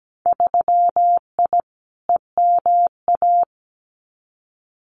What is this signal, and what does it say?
Morse code, heard as a keyed tone, about 17 wpm, 700 Hz. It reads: 3I EMA
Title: Winter Field Day Exchanges Encore 17wpm